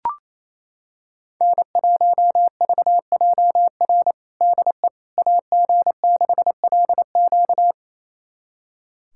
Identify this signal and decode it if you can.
Morse code, heard as a keyed tone, about 28 wpm, 700 Hz. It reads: N1VJR DE AG6LQ
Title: USA Callsign DE Callsign - 5 Characters - 28wpm